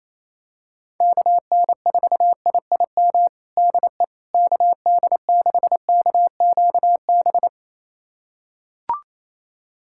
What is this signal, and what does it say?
Morse code, heard as a keyed tone, about 28 wpm, 700 Hz. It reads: KN4IIM DE KD6KQB
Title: USA Callsign DE Callsign 28wpm